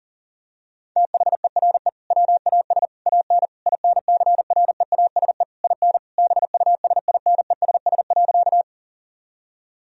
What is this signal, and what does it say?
Morse code, heard as a keyed tone, about 40 wpm, 700 Hz. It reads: THERE WAS AN INCREASE IN BUSINESS.
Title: Sentences from Top 1000 Words - Review Part 4 - 40wpm